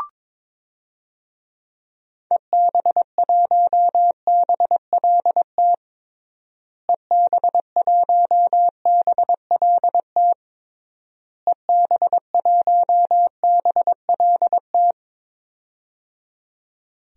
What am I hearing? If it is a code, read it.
Morse code, heard as a keyed tone, about 22 wpm, 700 Hz. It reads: EB1BLT EB1BLT EB1BLT
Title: DXCC-100 Call Signs - 6 Characters - Repeated 3x - 22wpm